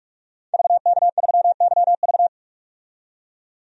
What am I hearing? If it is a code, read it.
Morse code, heard as a keyed tone, about 46 wpm, 700 Hz. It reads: VK3YV
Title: DXCC-100 Call Signs - 5 Characters - 45wpm